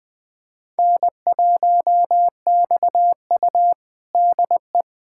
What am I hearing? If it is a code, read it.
Morse code, heard as a keyed tone, about 20 wpm, 700 Hz. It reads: N1XU DE
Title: USA Callsign DE Callsign - 4 Characters - 20wpm